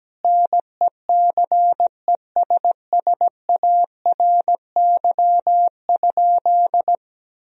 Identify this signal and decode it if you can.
Morse code, heard as a keyed tone, about 17 wpm, 700 Hz. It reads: NECESSARY?
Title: Sentences from Top 1000 Words - Review Part 3 - 17wpm